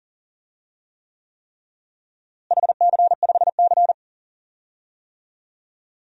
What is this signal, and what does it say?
Morse code, heard as a keyed tone, about 40 wpm, 700 Hz. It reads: HC5C
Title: DXCC-100 Call Signs - 4 Characters - Repeated 3x - 40wpm